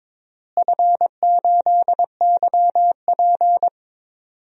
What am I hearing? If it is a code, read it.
Morse code, heard as a keyed tone, about 22 wpm, 700 Hz. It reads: F8YP
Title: DXCC-100 Call Signs - 4 Characters - Repeated 2x - 22wpm